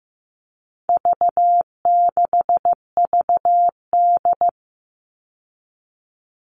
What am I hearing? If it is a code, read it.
Morse code, heard as a keyed tone, about 15 wpm, 700 Hz. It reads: V6VD
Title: All Cut Numbers - 4-Digit 15wpm